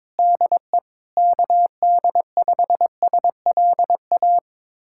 Morse code, heard as a keyed tone, about 22 wpm, 700 Hz. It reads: DE KD5SLA
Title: USA Callsign DE Callsign - 6 Characters - 22wpm